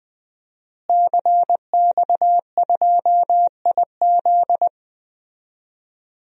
Morse code, heard as a keyed tone, about 20 wpm, 700 Hz. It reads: CX2IZ
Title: DXCC-100 Call Signs - 5 Characters - Repeated 2x - 20wpm